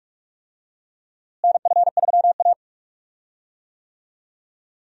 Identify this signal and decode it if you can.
Morse code, heard as a keyed tone, about 45 wpm, 700 Hz. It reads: NF3A
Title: USA Call Signs - 4 Characters - Repeated 2x - 45wpm